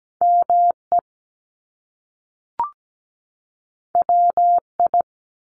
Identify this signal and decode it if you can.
Morse code, heard as a keyed tone, about 17 wpm, 700 Hz. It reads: ME WI
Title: US States 17wpm